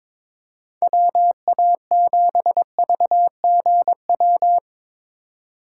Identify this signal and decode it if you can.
Morse code, heard as a keyed tone, about 22 wpm, 700 Hz. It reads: WA7VGW
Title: USA Call Signs - 6 Characters - 2x Character Spacing - 22wpm